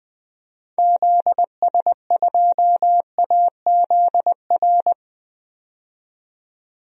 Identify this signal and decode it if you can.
Morse code, heard as a keyed tone, about 20 wpm, 700 Hz. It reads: ZS2AZR
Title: DXCC-100 Call Signs - 6 Characters - 20wpm